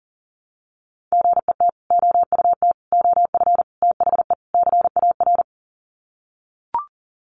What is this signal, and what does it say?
Morse code, heard as a keyed tone, about 40 wpm, 700 Hz. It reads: GET OUT OF THE CAR
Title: Sets of 5 Words Encore2 40wpm